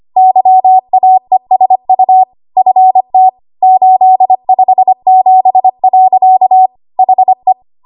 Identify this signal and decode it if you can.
Morse code, heard as a keyed tone, about 25 wpm, 750 Hz. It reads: YAESU FT 857. HE
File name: Archive / 25 WPM / 25WPM